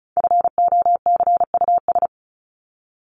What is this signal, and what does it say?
Morse code, heard as a keyed tone, about 34 wpm, 700 Hz. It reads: FOCUS
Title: Top 501-750 Words 35wpm